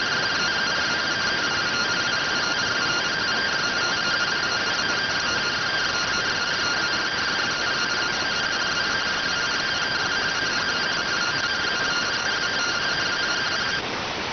Two-tone teleprinter signal, about 170 Hz shift.